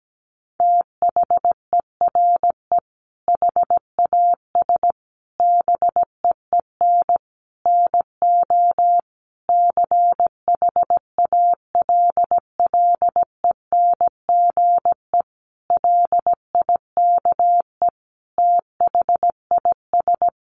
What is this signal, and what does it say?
Morse code, heard as a keyed tone, about 17 wpm, 700 Hz. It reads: THERE HAS BEEN NO CHALLENGE LIKE THIS
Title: Sentences from Top 1500 Words 17wpm